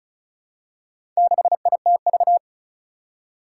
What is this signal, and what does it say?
Morse code, heard as a keyed tone, about 36 wpm, 700 Hz. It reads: 6ITV
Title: Four Challenging Letter-Number Combinations 35wpm